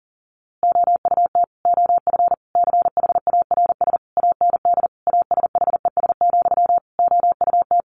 Morse code, heard as a keyed tone, about 40 wpm, 700 Hz. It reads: OUT OF CHARS AND ASHES, OUT
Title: A Sound Of Thunder 40wpm